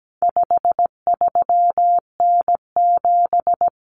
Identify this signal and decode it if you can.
Morse code, heard as a keyed tone, about 17 wpm, 700 Hz. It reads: 53N7
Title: Common Cut Numbers - 4-Digit 17wpm